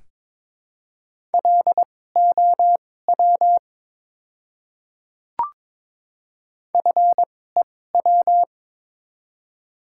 Morse code, heard as a keyed tone, about 22 wpm, spaced as for 16 wpm, 700 Hz. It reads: LOW FEW
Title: 2x Character Spacing - Top 300 Words - 22wpm